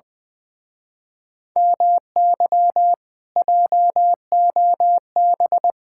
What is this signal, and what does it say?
Morse code, heard as a keyed tone, about 24 wpm, 700 Hz. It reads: MY JOB
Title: Speed-Racing Common QSO Phrases - 20wpm